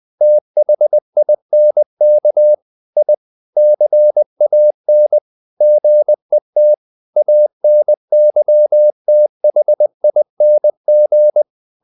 Morse code, heard as a keyed tone, about 20 wpm, 600 Hz. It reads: THINK I CAN GET ANYTHING
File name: the old man and the sea / the old man and the sea part 1 20wpm